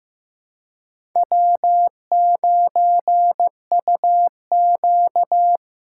Morse code, heard as a keyed tone, about 15 wpm, 700 Hz. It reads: W9UQ
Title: USA Callsign DE Callsign - 4 Characters - 15wpm